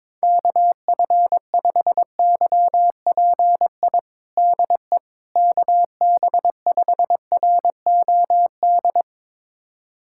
Morse code, heard as a keyed tone, about 22 wpm, 700 Hz. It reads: KF5YPI DE KB5ROD
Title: USA Callsign DE Callsign 22wpm